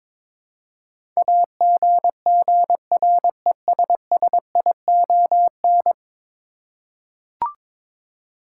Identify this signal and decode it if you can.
Morse code, heard as a keyed tone, about 22 wpm, 700 Hz. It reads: AGGRESSION
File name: practice / 22wpm / 10 Letter Words - First 3 Letters Given - 22wpm